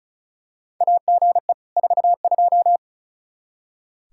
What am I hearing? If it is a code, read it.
Morse code, heard as a keyed tone, about 36 wpm, 700 Hz. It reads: AGE 42
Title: Practice QSO - Exchange 3 - More Info - 35wpm